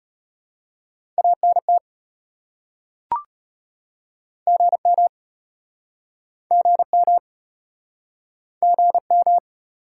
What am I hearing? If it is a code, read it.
Morse code, heard as a keyed tone, about 36 wpm, 690 Hz. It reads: ANT GM GM GM